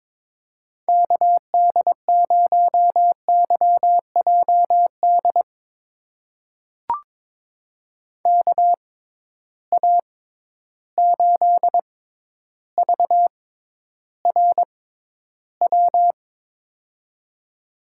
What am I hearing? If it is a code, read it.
Morse code, heard as a keyed tone, about 22 wpm, spaced as for 11 wpm, 700 Hz. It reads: KD0YJD KA8VRW
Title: USA Call Signs - 6x Character Spacing - 22wpm